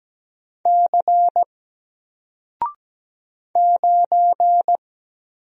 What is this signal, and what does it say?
Morse code, heard as a keyed tone, about 17 wpm, 700 Hz. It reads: C 9
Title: Single Character (Letter:Number:Punctuation:Prosign) 17wpm